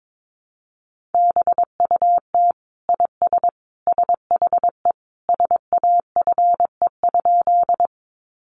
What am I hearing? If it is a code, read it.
Morse code, heard as a keyed tone, about 22 wpm, 700 Hz. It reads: BUT IS SHE SAFE?
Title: Sentences from Top 900 Words 22wpm